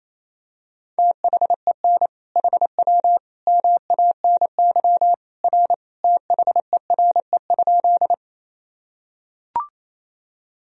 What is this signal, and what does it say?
Morse code, heard as a keyed tone, about 28 wpm, 700 Hz. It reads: THEN HW MANY R THERE?
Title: Sentences from Top 100 Words and Abbreviations 28wpm